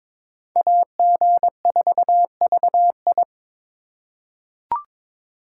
Morse code, heard as a keyed tone, about 23 wpm, 700 Hz. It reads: AG4VI
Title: USA Call Signs - 5 Characters - 4x Character Spacing - 22wpm